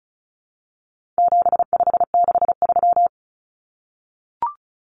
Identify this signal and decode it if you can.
Morse code, heard as a keyed tone, about 34 wpm, 700 Hz. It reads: 7563
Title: Numbers - 4-Digit 35wpm